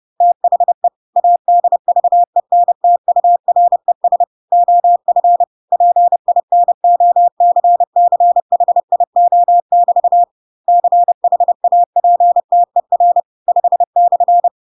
Morse code, heard as a keyed tone, about 30 wpm, 700 Hz. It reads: THE ADVENTURES OF PINOCCHIO<BT> CHAPTER 5/